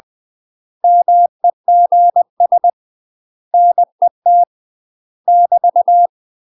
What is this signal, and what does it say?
Morse code, heard as a keyed tone, about 20 wpm, 700 Hz. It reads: MEGS NET <BT>
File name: UK MASTER QSOs TRAINING/20WPM